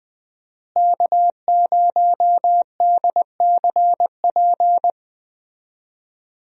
Morse code, heard as a keyed tone, about 20 wpm, 700 Hz. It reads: K0DCP
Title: USA Call Signs - 5 Characters - Repeated 2x - 20wpm